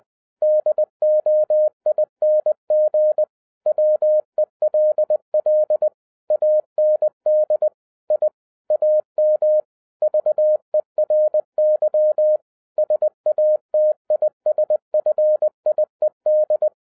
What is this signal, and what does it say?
Morse code, heard as a keyed tone, about 20 wpm, 610 Hz. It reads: DOING WELL AND I AM VERY SATISFIED